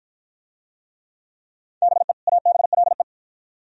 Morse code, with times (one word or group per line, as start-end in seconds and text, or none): 1.8-2.1 BE
2.3-3.0 ABLE